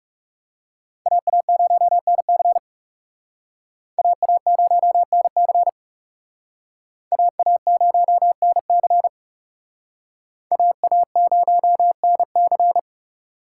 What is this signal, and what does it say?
Morse code, heard as a keyed tone, about 36 wpm, 700 Hz. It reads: AA0NC AA0NC AA0NC AA0NC
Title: Speed Racing - USA Call Signs 30wpm